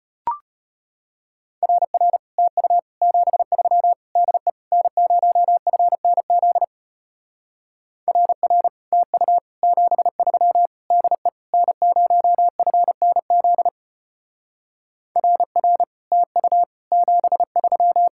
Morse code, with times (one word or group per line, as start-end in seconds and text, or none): 1.6-2.2 RR
2.4-2.8 TU
3.0-3.9 73
4.2-4.5 DE
4.7-6.6 N0FNZ
8.1-8.7 RR
8.9-9.4 TU
9.6-10.7 73
10.9-11.3 DE
11.5-13.7 N0FNZ
15.2-15.8 RR
16.1-16.6 TU
16.9-18.1 73